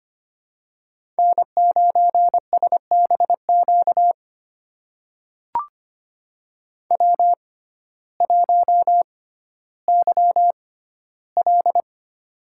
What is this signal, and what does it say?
Morse code, heard as a keyed tone, about 25 wpm, spaced as for 14 wpm, 700 Hz. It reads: N9SBQ W1YL
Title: USA Call Signs - 6x Character Spacing - 25wpm